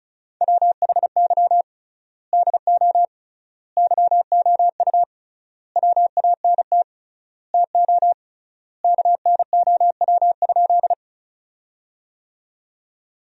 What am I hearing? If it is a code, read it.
Morse code, heard as a keyed tone, about 36 wpm, 700 Hz. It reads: WHY DO YOU WANT TO KNOW?
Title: 3x Word-Spacing - Sentences from Top 200 Words - 35wpm